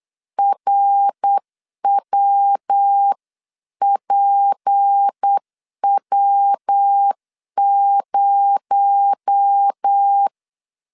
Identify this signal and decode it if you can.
Morse code, heard as a keyed tone, about 8 wpm, 800 Hz. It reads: RWPW0
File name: CW / DET / 1 6 1500C S C 8wpm 5219